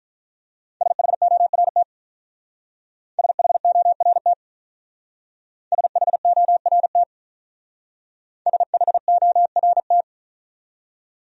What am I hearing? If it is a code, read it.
Morse code, heard as a keyed tone, about 44 wpm, 700 Hz. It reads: SHORT SHORT SHORT SHORT